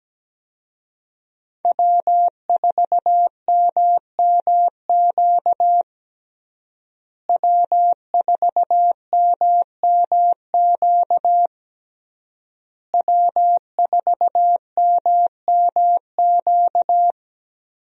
Morse code, heard as a keyed tone, about 17 wpm, 700 Hz. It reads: W4MMQ W4MMQ W4MMQ
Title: USA Call Signs - 5 Characters - Repeated 3x - 17wpm